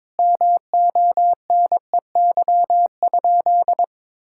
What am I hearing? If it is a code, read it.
Morse code, heard as a keyed tone, about 22 wpm, 700 Hz. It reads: MONEY?
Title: Sentences from Top 1000 Words - Review Part 5 - 22wpm